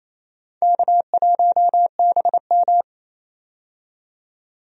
Morse code, heard as a keyed tone, about 27 wpm, 700 Hz. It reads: K1BM
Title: USA Call Signs - 4 Characters - 28wpm